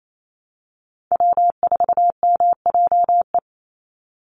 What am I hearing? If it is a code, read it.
Morse code, heard as a keyed tone, about 28 wpm, 700 Hz. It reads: W4MJE
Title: Call Signs Repeated 2x - 28wpm